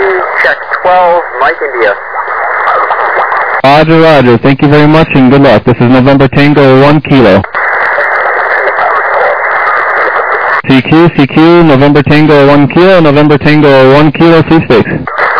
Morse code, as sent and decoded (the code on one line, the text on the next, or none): none